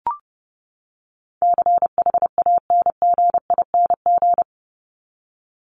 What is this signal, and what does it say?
Morse code, heard as a keyed tone, about 30 wpm, 700 Hz. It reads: CHANGING
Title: ING Suffix - Top 100 Words with Base Words 30wpm